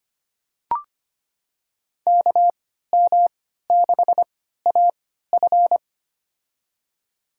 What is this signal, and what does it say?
Morse code, heard as a keyed tone, about 25 wpm, spaced as for 14 wpm, 700 Hz. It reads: KM6AF
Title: USA Call Signs - 5 Characters - 3x Character Spacing - 25wpm